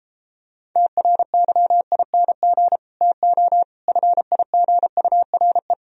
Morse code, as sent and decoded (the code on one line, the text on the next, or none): - .-. -.-- .. -. --. / - --- / ..-. .. --. ..- .-. .
TRYING TO FIGURE